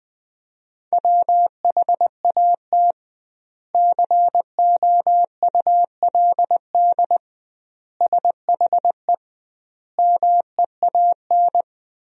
Morse code, heard as a keyed tone, about 19 wpm, 700 Hz. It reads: WHAT COULD SHE MEAN
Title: 2x Word-Spacing - Sentences from Top 200 Words - 20wpm